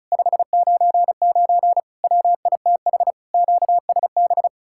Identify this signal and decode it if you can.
Morse code, heard as a keyed tone, about 46 wpm, 700 Hz. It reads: 599 WITH QSB